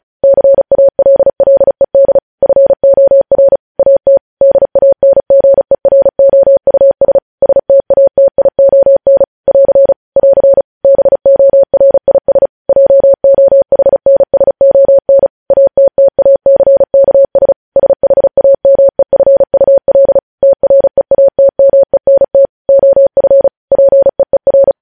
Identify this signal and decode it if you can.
Morse code, heard as a keyed tone, about 36 wpm, 550 Hz. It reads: CALLED FOR AT DANGEROUS STATION <AR> <AR> BORIS JOHNSON ATTACKS SHAMEFUL TREATMENT OF PEER